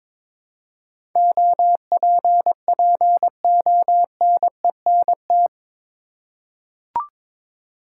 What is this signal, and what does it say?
Morse code, heard as a keyed tone, about 22 wpm, 700 Hz. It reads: OPPONENT
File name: practice / 22wpm / Eight Letter Words - 22wpm